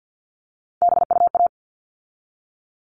Morse code, heard as a keyed tone, about 50 wpm, 700 Hz. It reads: BUA
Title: All Cut Numbers - 3-Digit 50wpm